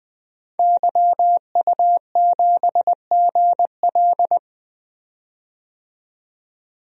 Morse code, heard as a keyed tone, about 20 wpm, 700 Hz. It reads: YU7GL
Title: DXCC-100 Call Signs - 5 Characters - 20wpm